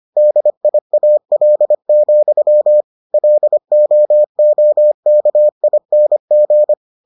Morse code, heard as a keyed tone, about 25 wpm, 600 Hz. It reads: DIAL, LOOKING